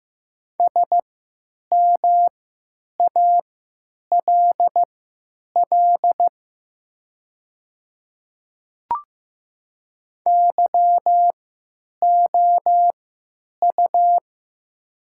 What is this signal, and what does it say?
Morse code, heard as a keyed tone, about 15 wpm, spaced as for 9 wpm, 700 Hz. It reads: SMALL YOU
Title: 3x Character Spacing - Top 200 Words - 15wpm